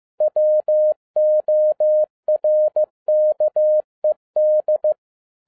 Morse code, heard as a keyed tone, about 15 wpm, 610 Hz. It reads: WORKED
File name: 2011-10-mp3 / n12-JL3YMV 15wpm